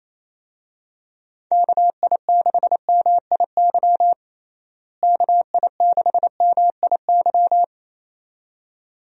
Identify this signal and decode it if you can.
Morse code, heard as a keyed tone, about 27 wpm, 700 Hz. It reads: KI6MIY KI6MIY
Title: USA Call Signs - 6 Characters - Repeated 2x - 28wpm